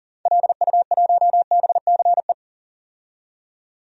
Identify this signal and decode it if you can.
Morse code, heard as a keyed tone, about 40 wpm, 700 Hz. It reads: LU1BCE
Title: DXCC-100 Call Signs - 6 Characters - 40wpm